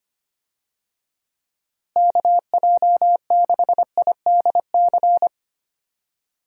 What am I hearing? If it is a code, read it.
Morse code, heard as a keyed tone, about 25 wpm, 700 Hz. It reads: KJ6IDC
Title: USA Call Signs - 6 Characters - Repeated 2x - 25wpm